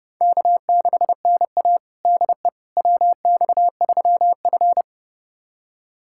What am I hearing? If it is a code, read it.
Morse code, heard as a keyed tone, about 31 wpm, 700 Hz. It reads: K6NA DE WX3F
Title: USA Callsign DE Callsign - 4 Characters - 30wpm